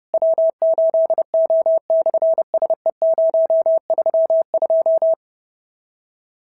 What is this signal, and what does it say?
Morse code, heard as a keyed tone, about 30 wpm, 650 Hz. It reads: W8O/SE032
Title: SOTA QSO Practice 30wpm